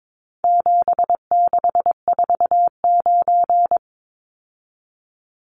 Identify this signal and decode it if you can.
Morse code, heard as a keyed tone, about 22 wpm, 700 Hz. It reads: 7649
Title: Numbers - 4-Digit 22wpm